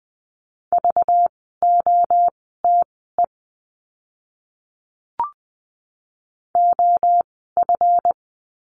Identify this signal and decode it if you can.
Morse code, heard as a keyed tone, about 20 wpm, spaced as for 14 wpm, 700 Hz. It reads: VOTE OF
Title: 192 Lesson 31f - Letter v - Words 20wpm